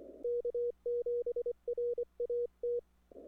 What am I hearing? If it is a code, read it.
Morse code, heard as a keyed tone, about 24 wpm, 480 Hz. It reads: K7RAT
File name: n6tr / w7dra